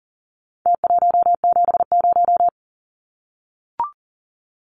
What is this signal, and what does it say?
Morse code, heard as a keyed tone, about 40 wpm, 700 Hz. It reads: T170
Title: Numbers - Contest Style - 4-Digit 40wpm